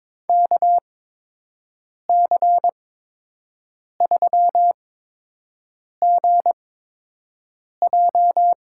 Morse code, heard as a keyed tone, about 22 wpm, spaced as for 6 wpm, 700 Hz. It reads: KC3GJ
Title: USA Call Signs - 6 Characters - 8x Character Spacing - 22wpm